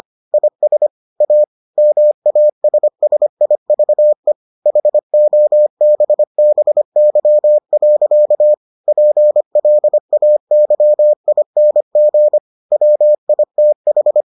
Morse code, heard as a keyed tone, about 25 wpm, 600 Hz. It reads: IS A MASSIVE HOBBY. PLAYING WITH